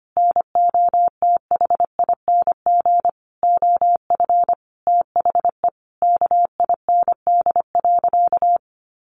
Morse code, heard as a keyed tone, about 25 wpm, 700 Hz. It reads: NOTHING OF THE KIND.
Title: Sentences from Top 1200 Words 25wpm